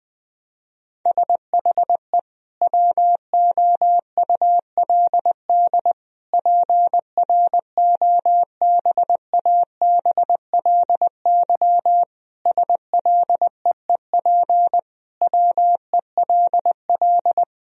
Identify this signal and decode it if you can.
Morse code, heard as a keyed tone, about 20 wpm, 700 Hz. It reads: SHE WOULD PROBABLY SLEEP WELL
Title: Sentences from Top 1000 Words - Review Part 4 - 20wpm